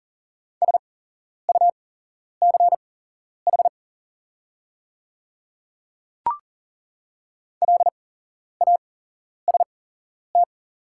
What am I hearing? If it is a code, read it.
Morse code, heard as a keyed tone, about 40 wpm, spaced as for 11 wpm, 700 Hz. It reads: SUCH LAST
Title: 8x Character Spacing - Top 300 Words - 40wpm